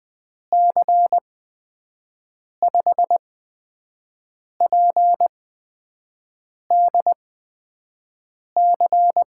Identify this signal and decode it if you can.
Morse code, heard as a keyed tone, about 20 wpm, spaced as for 5 wpm, 700 Hz. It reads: C5PDC
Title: USA Call Signs - 6 Characters - 8x Character Spacing - 20wpm